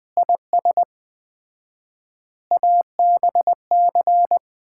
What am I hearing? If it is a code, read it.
Morse code, heard as a keyed tone, about 21 wpm, 700 Hz. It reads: IS ABC
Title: 4x Word-Spacing - Sets of 5 Words - 20wpm